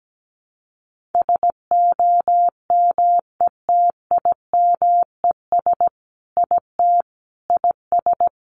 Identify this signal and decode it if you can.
Morse code, heard as a keyed tone, about 17 wpm, 700 Hz. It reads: SOMETIMES IT IS
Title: Sentences from Top 600 Words 17wpm